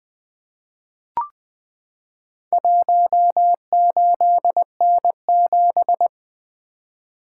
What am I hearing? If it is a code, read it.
Morse code, heard as a keyed tone, about 20 wpm, 700 Hz. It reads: 18N7
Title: Four Letter-Number Combinations - 20wpm